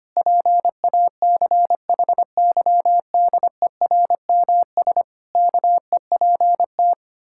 Morse code, heard as a keyed tone, about 25 wpm, 700 Hz. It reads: PACHYDERMS KEPT
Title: Pangram Challenge v2 - 25wpm